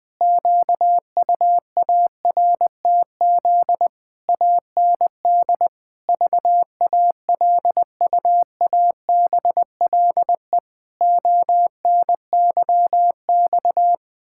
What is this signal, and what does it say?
Morse code, heard as a keyed tone, about 20 wpm, 700 Hz. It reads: QUARTZ AND VALUABLE ONYX
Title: Pangram Challenge v2 - 20wpm